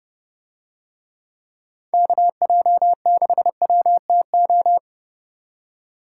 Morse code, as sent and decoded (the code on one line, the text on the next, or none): -.- .--- -.... .-- - ---
KJ6WTO